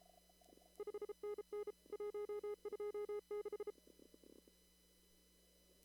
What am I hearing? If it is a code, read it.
Morse code, heard as a keyed tone, about 34 wpm, 420 Hz. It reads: 5NN 126
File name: CW / kt5j wpx cw-op-k5pi